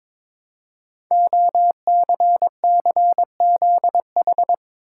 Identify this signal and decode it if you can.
Morse code, heard as a keyed tone, about 22 wpm, 700 Hz. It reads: OCCZH
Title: Code Groups - 5 Letters - 22wpm